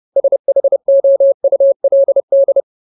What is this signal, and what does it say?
Morse code, heard as a keyed tone, about 30 wpm, 550 Hz. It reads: SHOULD
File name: Ultimate Word List / 30WPM Ultimate